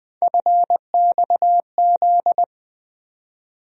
Morse code, heard as a keyed tone, about 20 wpm, 700 Hz. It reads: FXZ
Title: USA Call Sign Postfixes - 3 Letters 20wpm